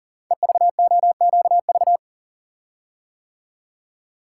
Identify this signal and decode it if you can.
Morse code, heard as a keyed tone, about 40 wpm, 700 Hz. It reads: EVOQV
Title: Code Groups - 5 Letters - 40wpm